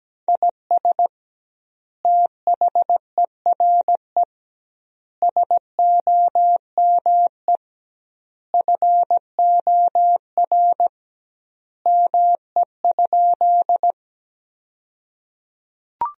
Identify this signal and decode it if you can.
Morse code, heard as a keyed tone, about 17 wpm, 700 Hz. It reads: IS THERE SOME FOR ME?
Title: 2x Word-Spacing - Sentences from Top 200 Words - 17wpm